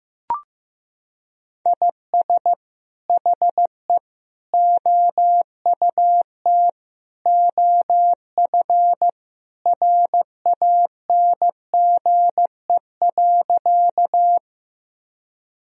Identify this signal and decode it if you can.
Morse code, heard as a keyed tone, about 15 wpm, 700 Hz. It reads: IS HE OUT OF RANGE.
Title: Sentences from Top 1000 Words - Review Part 4 - 15wpm